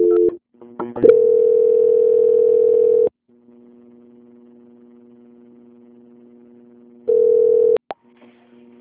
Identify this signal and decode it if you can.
Ringback tone (North America), 440 + 480 Hz.